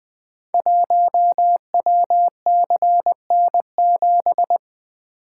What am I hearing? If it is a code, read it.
Morse code, heard as a keyed tone, about 20 wpm, 700 Hz. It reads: 1WCN7
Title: Code Groups - 5 Letter:Numbers - 20wpm